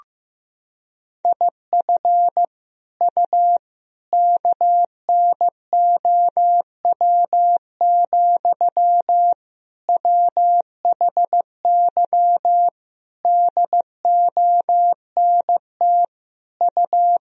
Morse code, heard as a keyed tone, about 15 wpm, 700 Hz. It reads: IF U KNOW, WHY DONT U
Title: Sentences from Top 300 Words and Abbreviations 15wpm